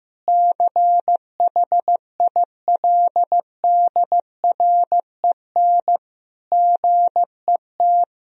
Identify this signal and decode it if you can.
Morse code, heard as a keyed tone, about 15 wpm, 700 Hz. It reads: CHILDREN GET
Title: Sentences from Top 1000 Words - Review Part 1 - 15wpm